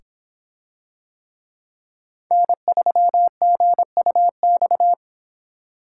Morse code, heard as a keyed tone, about 26 wpm, 700 Hz. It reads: N3GUX